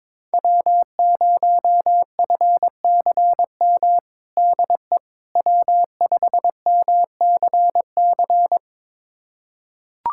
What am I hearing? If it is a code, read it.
Morse code, heard as a keyed tone, about 22 wpm, 700 Hz. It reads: W0FCM DE W5MCC
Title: USA Callsign DE Callsign - 5 Characters - 22wpm